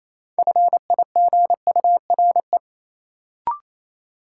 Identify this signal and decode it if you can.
Morse code, heard as a keyed tone, about 28 wpm, 700 Hz. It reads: FIGURE
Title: Sets of 4 Words - New Word Drill - 28wpm